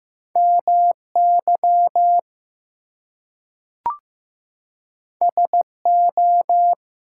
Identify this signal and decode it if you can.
Morse code, heard as a keyed tone, about 15 wpm, 700 Hz. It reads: MY SO
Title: Two Letter Words - 15wpm